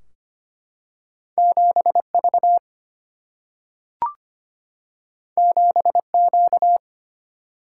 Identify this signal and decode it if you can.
Morse code, heard as a keyed tone, about 25 wpm, 700 Hz. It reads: 7V 7Q
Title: Single-Letter Single-Number 25wpm